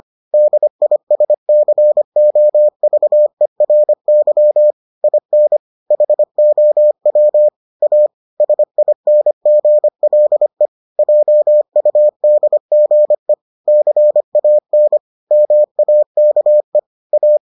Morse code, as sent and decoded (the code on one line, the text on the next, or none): -.. .. ... -.-. --- ...- . .-. -.-- / .. -. / .... --- .-- / .- / ... .. -. --. .-.. / .--- ..- -.. --. / -.-. .- -. / -- .- -.- . / .-
DISCOVERY IN HOW A SINGLE JUDGE CAN MAKE A